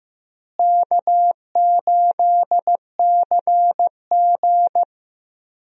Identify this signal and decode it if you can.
Morse code, heard as a keyed tone, about 15 wpm, 700 Hz. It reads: K8CG
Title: USA Call Signs - 4 Characters - Repeated 3x - 15wpm